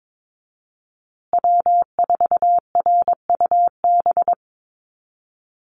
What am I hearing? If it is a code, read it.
Morse code, heard as a keyed tone, about 22 wpm, 700 Hz. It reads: W4RUB
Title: USA Call Signs Repeated 2x 22wpm